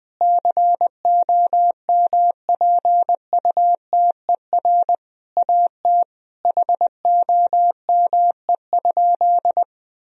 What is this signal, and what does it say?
Morse code, heard as a keyed tone, about 20 wpm, 700 Hz. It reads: COMPUTER AT HOME?
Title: Sentences from Top 1000 Words - Review Part 3 - 20wpm